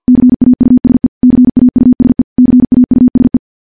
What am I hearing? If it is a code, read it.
Morse code, heard as a keyed tone, about 50 wpm, 250 Hz. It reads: CAUSE CAUSE CAUSE
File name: morse / morse-training-50WPM